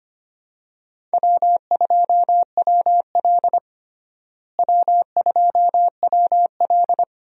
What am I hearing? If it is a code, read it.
Morse code, heard as a keyed tone, about 25 wpm, 700 Hz. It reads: W2WL W2WL
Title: USA Call Signs - 4 Characters - Repeated 3x - 25wpm